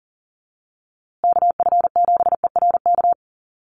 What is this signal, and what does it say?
Morse code, heard as a keyed tone, about 40 wpm, 700 Hz. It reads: KF7ERK
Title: USA Call Signs Repeated 3x 40wpm